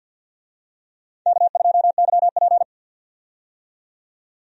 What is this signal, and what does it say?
Morse code, heard as a keyed tone, about 51 wpm, 700 Hz. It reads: K2YP
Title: USA Call Signs - 4 Characters - 50wpm